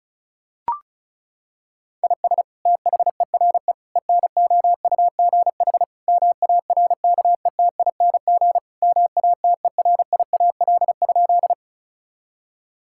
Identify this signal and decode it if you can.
Morse code, heard as a keyed tone, about 36 wpm, 700 Hz. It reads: IS THERE ENOUGH MARKETING MATERIAL?
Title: Sentences from Top 1000 Words - Review Part 5 - 35wpm